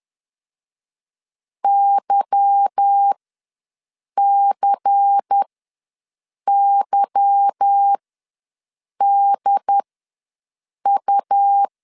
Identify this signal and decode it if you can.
Morse code, heard as a keyed tone, about 10 wpm, spaced as for 6 wpm, 800 Hz. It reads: YCYDU
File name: LEZIONI / 8 YJ45 / 8 Y ws 10wpm 3952